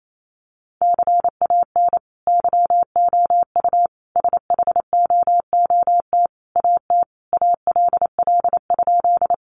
Morse code, heard as a keyed tone, about 28 wpm, 700 Hz. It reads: CAN YOU SHOOT AT ALL?
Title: Sentences from Top 1200 Words 28wpm